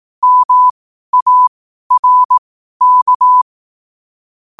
Morse code, heard as a keyed tone, about 18 wpm, spaced as for 12 wpm, 1000 Hz. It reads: MARK
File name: 12wpm